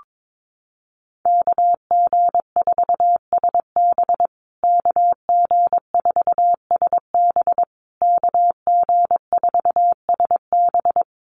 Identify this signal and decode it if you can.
Morse code, heard as a keyed tone, about 22 wpm, 700 Hz. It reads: KG4SB KG4SB KG4SB
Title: USA Call Signs Repeated 3x 22wpm